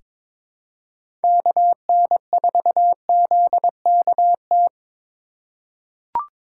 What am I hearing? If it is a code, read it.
Morse code, heard as a keyed tone, about 22 wpm, 700 Hz. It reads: KN4ZKT
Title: K1USN SST QSO Practice - 22wpm